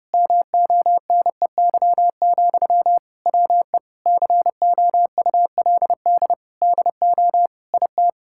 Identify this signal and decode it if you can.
Morse code, heard as a keyed tone, about 30 wpm, 700 Hz. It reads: MONEY, WE COULD DO IT
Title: Sentences from Top 1600 Words 30wpm